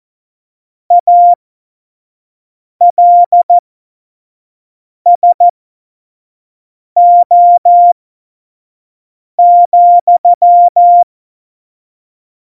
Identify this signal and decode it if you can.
Morse code, heard as a keyed tone, about 14 wpm, spaced as for 5 wpm, 700 Hz. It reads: ALSO,
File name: Morsekurs / Witze / Witz 04 August 2020, 25 F70